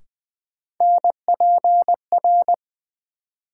Morse code, heard as a keyed tone, about 20 wpm, 700 Hz. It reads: NPR
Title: Sets of 5 Words - New Word Drill - 20wpm